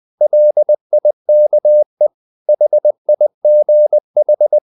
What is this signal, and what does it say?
Morse code, heard as a keyed tone, about 20 wpm, 600 Hz. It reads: LIKE HIGH
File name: the old man and the sea / the old man and the sea part 1 20wpm